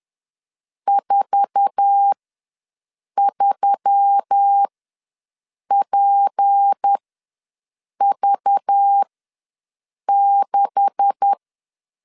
Morse code, heard as a keyed tone, about 10 wpm, spaced as for 6 wpm, 800 Hz. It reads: 43PV6
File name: LEZIONI / 9 2867 ws 10wpm 4215